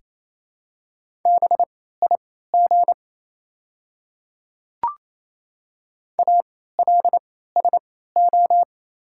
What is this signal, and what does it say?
Morse code, heard as a keyed tone, about 28 wpm, spaced as for 16 wpm, 700 Hz. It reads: BIG ALSO
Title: 3x Character Spacing - Top 200 Words - 28wpm